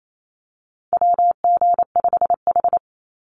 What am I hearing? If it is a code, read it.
Morse code, heard as a keyed tone, about 28 wpm, 700 Hz. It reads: WG5H
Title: Call Signs - 28wpm